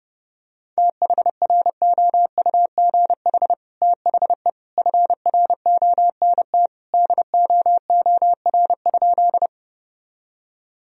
Morse code, heard as a keyed tone, about 30 wpm, 700 Hz. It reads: THROUGH THE FRONT DOOR?
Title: Sentences from Top 1000 Words - Review Part 2 - 30wpm